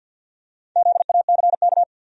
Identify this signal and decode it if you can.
Morse code, heard as a keyed tone, about 51 wpm, 700 Hz. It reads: ZACK
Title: USA Common Names - 50wpm